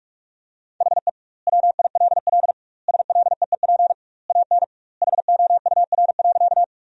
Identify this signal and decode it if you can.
Morse code, heard as a keyed tone, about 46 wpm, 700 Hz. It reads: HE WILL SLEEP AN HOUR.
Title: 2x Word-Spacing - Sentences from Top 500 Words - 45wpm